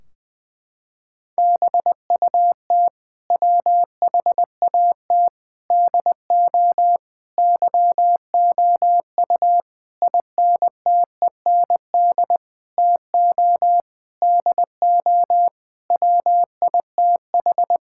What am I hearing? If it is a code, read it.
Morse code, heard as a keyed tone, about 21 wpm, 700 Hz. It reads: BUT WHAT DO YOU INTEND TO DO WITH
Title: Sentences from Top 1600 Words 20wpm